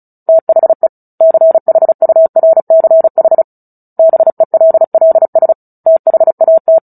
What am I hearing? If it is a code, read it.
Morse code, heard as a keyed tone, about 34 wpm, 650 Hz. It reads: THE CHURCH BELLS THAT
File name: Book Two / 09-Chapter 9 - 35 WPM